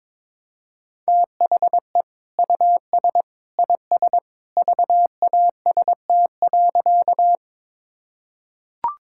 Morse code, heard as a keyed tone, about 22 wpm, 700 Hz. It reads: THE US IS VAST.
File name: practice / 22wpm / Sentences with 2-Word Phrases - 22wpm